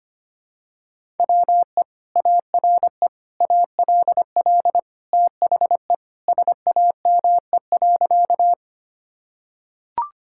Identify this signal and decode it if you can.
Morse code, heard as a keyed tone, about 25 wpm, 700 Hz. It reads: WE ARE ALL THE SAME.
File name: practice / 25wpm / Sentences with 2-Word Phrases - 25wpm